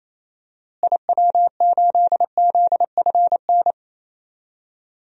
Morse code, heard as a keyed tone, about 27 wpm, 700 Hz. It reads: IW8ZFN
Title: DXCC-100 Call Signs - 6 Characters - Repeated 3x - 28wpm